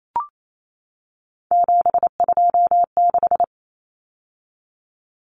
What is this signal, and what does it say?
Morse code, heard as a keyed tone, about 28 wpm, 700 Hz. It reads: 726
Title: Numbers - 3-Digit 28wpm